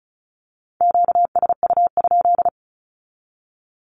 Morse code, heard as a keyed tone, about 34 wpm, 700 Hz. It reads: QSU?
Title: Q-Codes with Spoken Meaning - 35wpm